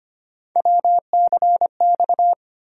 Morse code, heard as a keyed tone, about 25 wpm, 700 Hz. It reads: WCX
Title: USA Call Sign Postfixes - 3 Letters 25wpm